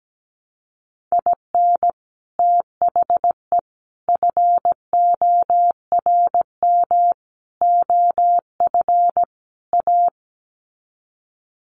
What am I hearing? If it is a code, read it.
Morse code, heard as a keyed tone, about 17 wpm, 700 Hz. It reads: IN THE FORM OF A
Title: Sets of 5 Words 17wpm